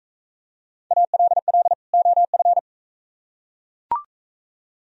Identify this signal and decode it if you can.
Morse code, heard as a keyed tone, about 42 wpm, 700 Hz. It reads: ALL OF